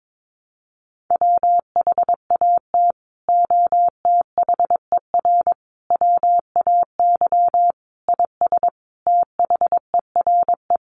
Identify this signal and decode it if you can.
Morse code, heard as a keyed tone, about 22 wpm, 700 Hz. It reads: WHAT OTHER WAY IS THERE
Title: Sentences from Top 100 Words 22wpm